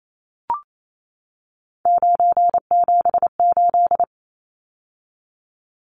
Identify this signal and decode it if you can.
Morse code, heard as a keyed tone, about 28 wpm, 700 Hz. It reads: 978
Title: Numbers - 3-Digit 28wpm